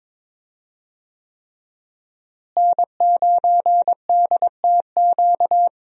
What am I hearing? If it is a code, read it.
Morse code, heard as a keyed tone, about 22 wpm, 700 Hz. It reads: N9DTQ
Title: USA Call Signs - 5 Characters - Repeated 3x - 22wpm